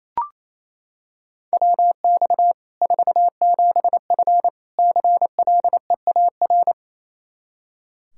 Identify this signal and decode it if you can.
Morse code, heard as a keyed tone, about 28 wpm, 700 Hz. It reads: WX 47F CLEAR
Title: Practice QSO - Exchange 2 - Info - 28wpm